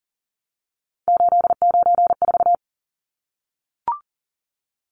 Morse code, heard as a keyed tone, about 40 wpm, 700 Hz. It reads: 894
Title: Numbers - 3-Digit 40wpm